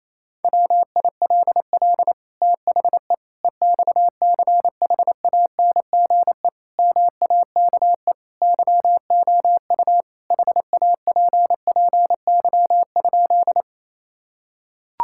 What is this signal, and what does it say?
Morse code, heard as a keyed tone, about 28 wpm, 690 Hz. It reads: WILL THE EXCHANGE MAKE YOU HAPPY?
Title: Sentences from Top 1000 Words - Review Part 6 - 28wpm